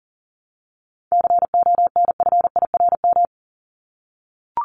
Morse code, heard as a keyed tone, about 40 wpm, 700 Hz. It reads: CONFIRM
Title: Top 1751-2000 Words 40wpm